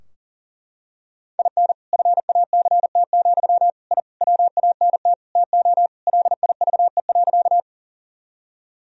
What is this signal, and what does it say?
Morse code, heard as a keyed tone, about 40 wpm, 700 Hz. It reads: IN FACT, I WANT TO LIVE.
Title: Sentences from Top 1000 Words - Review Part 3 - 40wpm